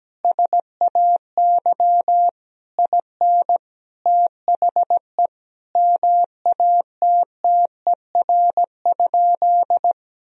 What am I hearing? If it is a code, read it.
Morse code, heard as a keyed tone, about 17 wpm, 700 Hz. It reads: SAY IN THE MATTER?
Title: Sentences from Top 1000 Words - Review Part 3 - 17wpm